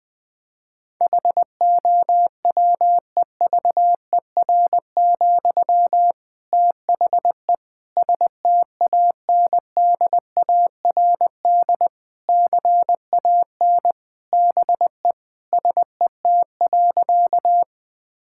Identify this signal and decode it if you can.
Morse code, heard as a keyed tone, about 20 wpm, 700 Hz. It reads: HOWEVER, THE STANDARD CAN BE SET.
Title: Sentences from Top 1000 Words - Review Part 6 - 20wpm